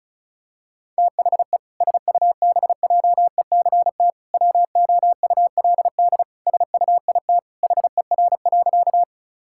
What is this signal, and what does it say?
Morse code, heard as a keyed tone, about 36 wpm, 700 Hz. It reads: THE SUBJECT WOULD SUIT HER.
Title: Sentences from Top 1600 Words 35wpm